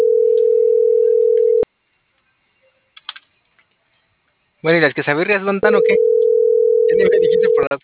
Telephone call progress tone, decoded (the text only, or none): ringback tone